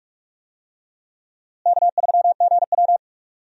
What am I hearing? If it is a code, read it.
Morse code, heard as a keyed tone, about 41 wpm, 700 Hz. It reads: K3GW
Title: USA Call Signs - 4 Characters - Repeated 3x - 45wpm